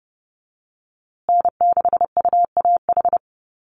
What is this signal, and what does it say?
Morse code, heard as a keyed tone, about 30 wpm, 700 Hz. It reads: N6UAH
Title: USA Call Signs Repeated 2x 30wpm